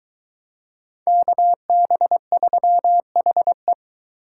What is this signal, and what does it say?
Morse code, heard as a keyed tone, about 22 wpm, 700 Hz. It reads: KB3HE